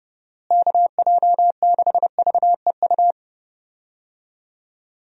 Morse code, heard as a keyed tone, about 30 wpm, 700 Hz. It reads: KJ6VEU
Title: USA Call Signs - 6 Characters - Repeated 3x - 30wpm